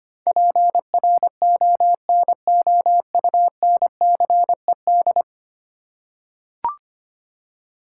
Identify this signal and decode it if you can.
Morse code, heard as a keyed tone, about 25 wpm, 700 Hz. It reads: PRONOUNCED
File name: 10 Letter Words - First 3 Letters Given - 25wpm